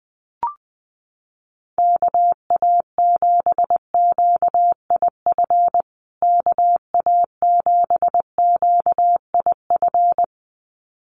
Morse code, heard as a keyed tone, about 20 wpm, 700 Hz. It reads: KA7QIF KA7QIF
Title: USA Call Signs Repeated 2x 20wpm